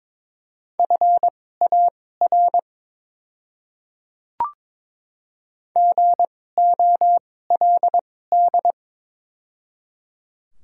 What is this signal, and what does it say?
Morse code, heard as a keyed tone, about 22 wpm, spaced as for 16 wpm, 700 Hz. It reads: FAR GOLD
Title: 2x Character Spacing - Top 500 Words - 22wpm